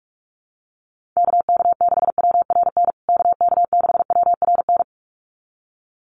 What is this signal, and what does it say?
Morse code, heard as a keyed tone, about 44 wpm, 700 Hz. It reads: KK6WRN KK6WRN
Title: USA Call Signs Repeated 2x 45wpm